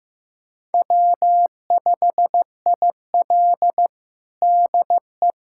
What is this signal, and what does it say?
Morse code, heard as a keyed tone, about 15 wpm, 700 Hz. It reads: W5IL DE
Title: USA Callsign DE Callsign - 4 Characters - 15wpm